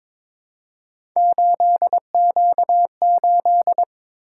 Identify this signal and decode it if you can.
Morse code, heard as a keyed tone, about 22 wpm, 700 Hz. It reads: 8Q8
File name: practice / 22wpm / Three Letter-Number Combinations - 22wpm